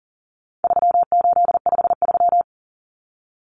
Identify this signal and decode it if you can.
Morse code, heard as a keyed tone, about 40 wpm, 700 Hz. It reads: 3853
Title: Common Cut Numbers - 4-Digit 40wpm